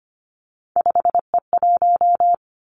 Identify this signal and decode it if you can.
Morse code, heard as a keyed tone, about 25 wpm, 700 Hz. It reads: 5E1
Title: Three Challenging Letter-Number Combinations 25wpm